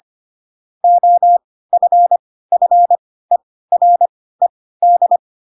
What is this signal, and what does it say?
Morse code, heard as a keyed tone, about 25 wpm, spaced as for 16 wpm, 700 Hz. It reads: OFFERED
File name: Pinocchio / Pinocchio ch7 10 25 ik1hge